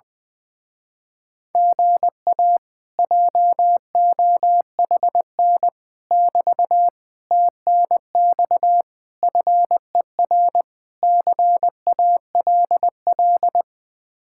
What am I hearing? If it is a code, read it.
Morse code, heard as a keyed tone, about 20 wpm, 700 Hz. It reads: GA JOHN <BT> TNX FER CALL
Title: Speed-Racing Common QSO Phrases Encore - 20wpm